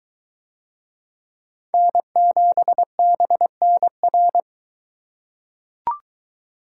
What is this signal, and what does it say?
Morse code, heard as a keyed tone, about 23 wpm, 700 Hz. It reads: N7BNR